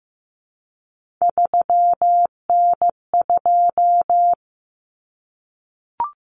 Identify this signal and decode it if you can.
Morse code, heard as a keyed tone, about 15 wpm, 700 Hz. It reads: 3N2
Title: Common Cut Numbers - 3-Digit 15wpm